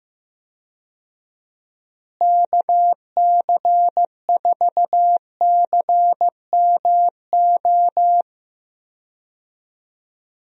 Morse code, heard as a keyed tone, about 15 wpm, 700 Hz. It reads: KC4CMO
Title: USA Call Signs - 6 Characters - Repeated 3x - 15wpm